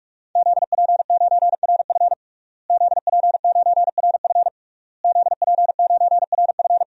Morse code, heard as a keyed tone, about 46 wpm, 700 Hz. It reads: ZP9RF ZP9RF ZP9RF
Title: DXCC-100 Call Signs - 5 Characters - Repeated 3x - 45wpm